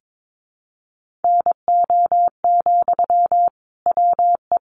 Morse code, heard as a keyed tone, about 22 wpm, 700 Hz. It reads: NO, WE
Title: Sentences from Top 1100 Words 22wpm